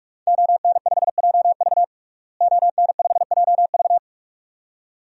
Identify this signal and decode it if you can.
Morse code, heard as a keyed tone, about 42 wpm, 700 Hz. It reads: ON5JV ON5JV
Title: DXCC-100 Call Signs - 5 Characters - Repeated 2x - 45wpm